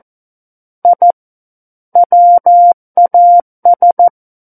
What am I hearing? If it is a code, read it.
Morse code, heard as a keyed tone, about 14 wpm, 700 Hz. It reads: I WAS
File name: Book One / 05-Chapter 5 - 14 WPM